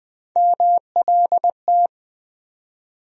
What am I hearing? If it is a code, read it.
Morse code, heard as a keyed tone, about 20 wpm, 700 Hz. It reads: MLT
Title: USA Call Sign Postfixes - 3 Letters 20wpm